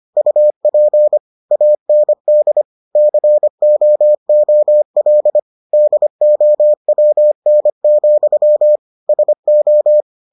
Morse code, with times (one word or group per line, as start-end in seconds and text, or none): 0.2-1.2 UP
1.5-2.6 AND
3.0-5.4 COOL
5.7-8.8 DOWN,
9.1-10.0 SO